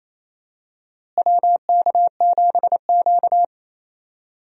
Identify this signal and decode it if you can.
Morse code, heard as a keyed tone, about 28 wpm, 700 Hz. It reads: WK7Q
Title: USA Call Signs - 4 Characters - Repeated 3x - 28wpm